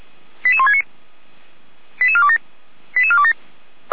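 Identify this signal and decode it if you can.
Five-tone selective call (ZVEI), heard as quick runs of five tones, tones of about 70 ms. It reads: 88169 88427 88427
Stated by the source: Five-tone ZVEI selective call received by SDR.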